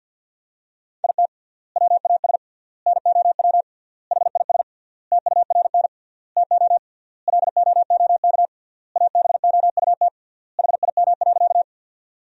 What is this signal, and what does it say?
Morse code, heard as a keyed tone, about 51 wpm, 700 Hz. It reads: IT WAS NOW HIS TURN TO LOOK ABOUT HIM.
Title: 3x Word-Spacing - Sentences from Top 200 Words - 50wpm